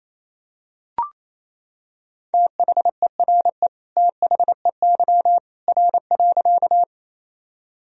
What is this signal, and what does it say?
Morse code, heard as a keyed tone, about 27 wpm, 700 Hz. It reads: THERE THEY R.
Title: Sentences from Top 100 Words and Abbreviations 28wpm